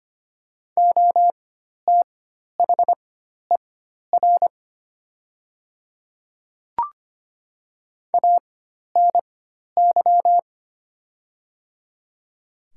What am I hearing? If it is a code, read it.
Morse code, heard as a keyed tone, about 25 wpm, spaced as for 12 wpm, 700 Hz. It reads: OTHER ANY
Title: 4x Character Spacing - Top 300 Words - 25wpm